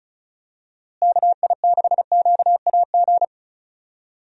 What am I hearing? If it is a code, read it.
Morse code, heard as a keyed tone, about 34 wpm, 700 Hz. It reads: KI6QAG
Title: USA Call Signs - 6 Characters - Repeated 3x - 35wpm